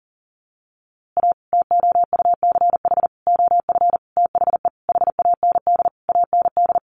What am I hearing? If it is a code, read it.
Morse code, heard as a keyed tone, about 40 wpm, 700 Hz. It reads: A TOUCH OF THE HAND AND
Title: A Sound Of Thunder 40wpm